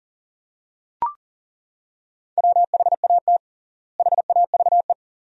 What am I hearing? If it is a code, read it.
Morse code, heard as a keyed tone, about 40 wpm, 700 Hz. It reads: WHAT HAVE
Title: 3x Word-Spacing - Sentences from Top 100 Words - 40wpm